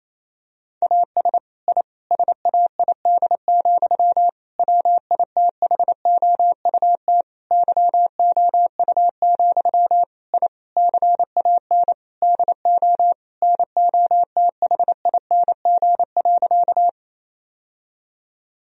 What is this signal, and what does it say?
Morse code, heard as a keyed tone, about 28 wpm, 700 Hz. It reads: AS I SAID, WITHOUT YOU, I CAN DO NOTHING.
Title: Sentences from Top 1000 Words - Review Part 3 - 28wpm